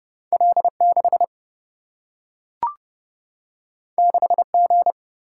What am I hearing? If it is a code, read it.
Morse code, heard as a keyed tone, about 30 wpm, 700 Hz. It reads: L6 6G
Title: Single-Letter Single-Number 30wpm